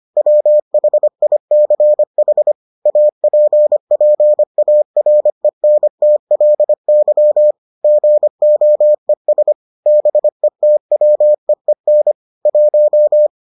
Morse code, heard as a keyed tone, about 25 wpm, 600 Hz. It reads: WHICH APPARENTLY GOES BETWEEN 1